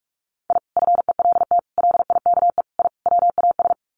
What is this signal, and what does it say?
Morse code, heard as a keyed tone, about 44 wpm, 700 Hz. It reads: I FELT LIKE I WAS
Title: Sets of 5 Words Encore 45wpm